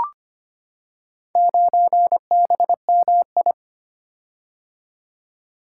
Morse code, heard as a keyed tone, about 25 wpm, 700 Hz. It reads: 9BMI
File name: Four Letter-Number Combinations - 25wpm